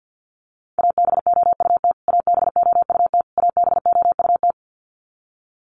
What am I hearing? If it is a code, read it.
Morse code, heard as a keyed tone, about 50 wpm, 700 Hz. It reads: ABOUT ABOUT ABOUT
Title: Top 500 Words Repeated 50wpm